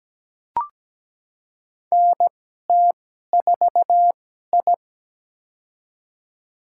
Morse code, heard as a keyed tone, about 17 wpm, spaced as for 12 wpm, 700 Hz. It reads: NT4I
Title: USA Call Signs - 4 Characters - 2x Character Spacing - 17wpm